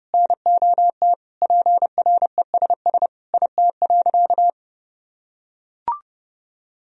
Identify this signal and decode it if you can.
Morse code, heard as a keyed tone, about 30 wpm, 700 Hz. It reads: NOT PRESS IT.
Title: Sentences from Top 1000 Words - Review Part 1 - 30wpm